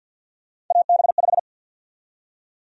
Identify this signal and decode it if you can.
Morse code, heard as a keyed tone, about 50 wpm, 700 Hz. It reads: AB5
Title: USA Call Sign Prefixes - 50wpm